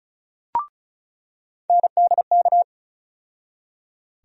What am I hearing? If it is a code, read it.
Morse code, heard as a keyed tone, about 36 wpm, 700 Hz. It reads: NDK
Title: USA Call Sign Postfixes - 3 Letters 35wpm